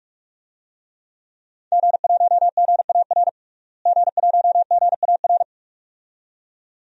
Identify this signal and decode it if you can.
Morse code, heard as a keyed tone, about 44 wpm, 700 Hz. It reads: G1GAR G1GAR
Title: DXCC-100 Call Signs - 5 Characters - Repeated 2x - 45wpm